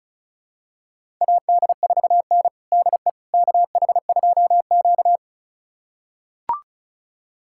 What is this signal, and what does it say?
Morse code, heard as a keyed tone, about 36 wpm, 700 Hz. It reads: AD4N DE KH2Q
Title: USA Callsign DE Callsign - 4 Characters - 35wpm